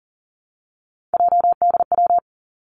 Morse code, heard as a keyed tone, about 40 wpm, 700 Hz. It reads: JDW
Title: Three Challenging Letter-Number Combinations 40wpm